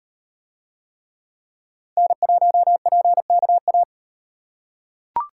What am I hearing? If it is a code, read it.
Morse code, heard as a keyed tone, about 36 wpm, 700 Hz. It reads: N1PKA